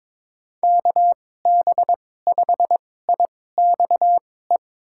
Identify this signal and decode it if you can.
Morse code, heard as a keyed tone, about 22 wpm, spaced as for 16 wpm, 700 Hz. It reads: KB5IXE
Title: USA Call Signs - 6 Characters - 2x Character Spacing - 22wpm